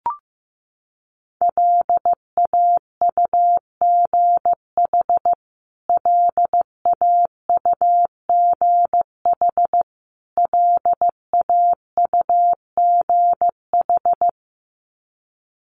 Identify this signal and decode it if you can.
Morse code, heard as a keyed tone, about 15 wpm, 700 Hz. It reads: LAUGH LAUGH LAUGH
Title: Top 500 Words Repeated 15wpm